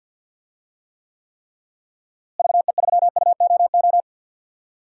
Morse code, heard as a keyed tone, about 47 wpm, 700 Hz. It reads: VE3UOO
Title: DXCC-100 Call Signs - 6 Characters - Repeated 3x - 50wpm